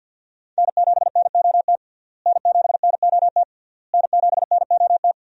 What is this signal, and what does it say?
Morse code, heard as a keyed tone, about 46 wpm, 700 Hz. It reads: N7NOT N7NOT N7NOT
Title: USA Call Signs - 5 Characters - Repeated 3x - 50wpm